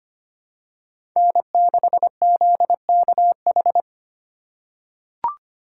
Morse code, heard as a keyed tone, about 25 wpm, 700 Hz. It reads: N6ZKH
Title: USA Call Signs - 5 Characters - 4x Character Spacing - 25wpm